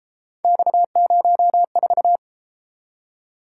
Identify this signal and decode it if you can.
Morse code, heard as a keyed tone, about 33 wpm, 700 Hz. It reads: X04